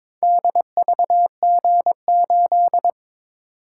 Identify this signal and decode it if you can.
Morse code, heard as a keyed tone, about 22 wpm, 700 Hz. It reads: DVG8
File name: Four Letter-Number Combinations - 22wpm